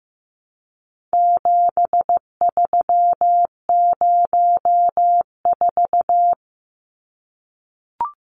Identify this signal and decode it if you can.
Morse code, heard as a keyed tone, about 15 wpm, 700 Hz. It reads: 7304
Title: Numbers - 4-Digit 15wpm